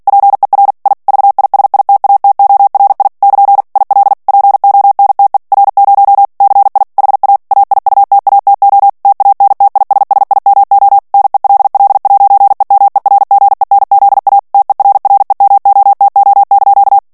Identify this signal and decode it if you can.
Morse code, heard as a keyed tone, about 48 wpm, 800 Hz. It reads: PER I VISITATORI <KN> IL PONTE R0 CI HA AIUTATO TANTISSIMO NELL'EMERGENZA TERREMOTO)
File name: morse mp3 / ITALIEN texte clair 1 vitesse 45 mots minute